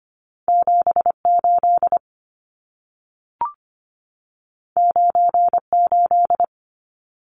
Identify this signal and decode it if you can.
Morse code, heard as a keyed tone, about 25 wpm, 700 Hz. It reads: 78 98
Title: Numbers - 2-Digit 25wpm